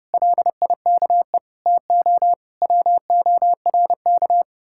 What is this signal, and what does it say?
Morse code, heard as a keyed tone, about 40 wpm, 700 Hz. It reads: LIKE TO WORK